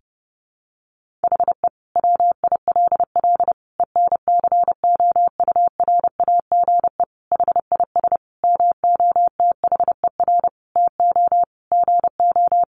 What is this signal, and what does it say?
Morse code, heard as a keyed tone, about 30 wpm, 700 Hz. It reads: HE WILL ENCOURAGE HIS MOTHER TO GO
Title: Sentences from Top 1400 Words 30wpm